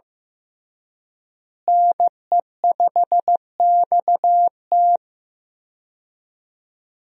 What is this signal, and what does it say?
Morse code, heard as a keyed tone, about 15 wpm, 700 Hz. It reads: NE5XT
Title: USA Call Signs - 5 Characters - Repeated 3x - 15wpm